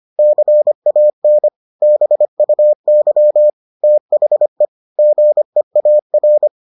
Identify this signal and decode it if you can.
Morse code, heard as a keyed tone, about 25 wpm, 600 Hz. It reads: CAN BUY THE GEAR